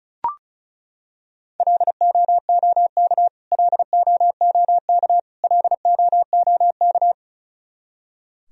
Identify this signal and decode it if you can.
Morse code, heard as a keyed tone, about 36 wpm, 700 Hz. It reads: LOOK LOOK LOOK
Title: Top 100 Words Repeated 35wpm